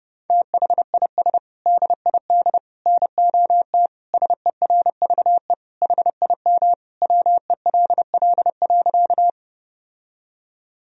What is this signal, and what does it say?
Morse code, heard as a keyed tone, about 31 wpm, 700 Hz. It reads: THIS DID NOT SERVE HIM WELL.
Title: Sentences from Top 1000 Words - Review Part 2 - 30wpm